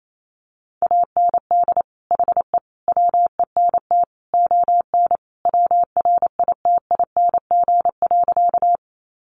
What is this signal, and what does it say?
Morse code, heard as a keyed tone, about 28 wpm, 700 Hz. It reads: AND HE WENT ON WRITING.
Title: Sentences from Top 800 Words 28wpm